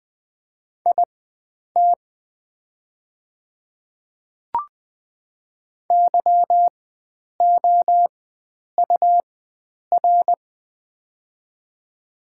Morse code, heard as a keyed tone, about 20 wpm, spaced as for 9 wpm, 700 Hz. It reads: IT YOUR
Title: 4x Character Spacing - Top 100 Words - 20wpm